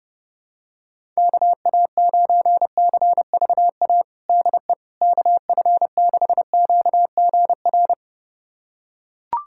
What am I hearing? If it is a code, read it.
Morse code, heard as a keyed tone, about 32 wpm, 700 Hz. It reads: KA9CVA DE KF6QGR
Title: USA Callsign DE Callsign 30wpm